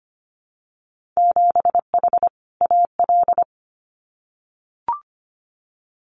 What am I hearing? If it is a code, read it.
Morse code, heard as a keyed tone, about 25 wpm, 700 Hz. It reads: 7H AL
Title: Winter Field Day Exchanges Encore 25wpm